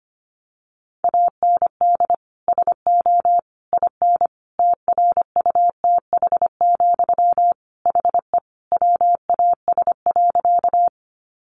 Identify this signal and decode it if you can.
Morse code, heard as a keyed tone, about 25 wpm, 700 Hz. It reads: AND SO IN TRUTH, HE WAS.
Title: Sentences from Top 800 Words 25wpm